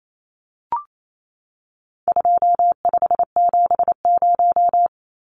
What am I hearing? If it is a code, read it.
Morse code, heard as a keyed tone, about 28 wpm, 700 Hz. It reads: 2570
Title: Numbers - 4-Digit 28wpm